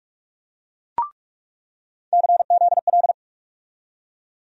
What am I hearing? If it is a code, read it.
Morse code, heard as a keyed tone, about 42 wpm, 700 Hz. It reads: CZL
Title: USA Call Sign Postfixes - 3 Letters 45wpm